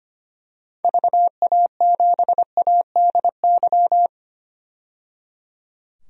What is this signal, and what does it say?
Morse code, heard as a keyed tone, about 25 wpm, 700 Hz. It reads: VA7ADY
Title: DXCC-100 Call Signs - 6 Characters - 25wpm